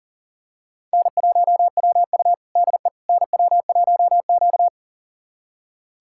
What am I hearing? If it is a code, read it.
Morse code, heard as a keyed tone, about 40 wpm, 700 Hz. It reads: N1WU DE NW1Q
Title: USA Callsign DE Callsign - 4 Characters - 40wpm